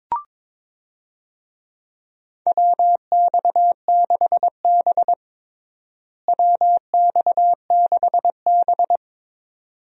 Morse code, heard as a keyed tone, about 22 wpm, 700 Hz. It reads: WX6B WX6B
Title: USA Call Signs - 4 Characters - Repeated 2x - 22wpm